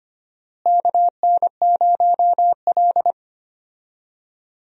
Morse code, heard as a keyed tone, about 25 wpm, 700 Hz. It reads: KN0L
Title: USA Call Signs - 4 Characters - Repeated 3x - 25wpm